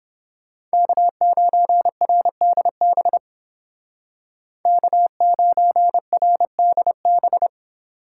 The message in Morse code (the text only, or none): K9RDB K9RDB